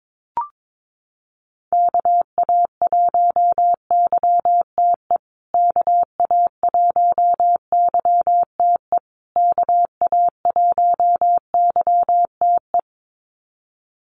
Morse code, heard as a keyed tone, about 22 wpm, 700 Hz. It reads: KA1YTE KA1YTE KA1YTE
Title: USA Call Signs Repeated 3x 22wpm